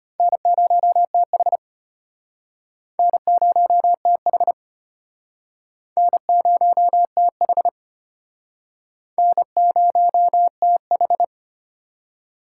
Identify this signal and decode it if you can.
Morse code, heard as a keyed tone, about 31 wpm, 700 Hz. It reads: N0TH N0TH N0TH N0TH
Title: Speed Racing - USA 1x2 Call Signs 25wpm